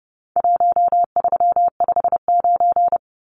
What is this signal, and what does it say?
Morse code, heard as a keyed tone, about 30 wpm, 700 Hz. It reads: 1359
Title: Numbers - Contest Style - 4-Digit 30wpm